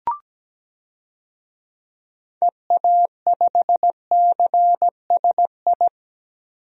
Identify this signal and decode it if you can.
Morse code, heard as a keyed tone, about 17 wpm, 700 Hz. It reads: EA5CSI
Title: DXCC-100 Call Signs - 6 Characters - 17wpm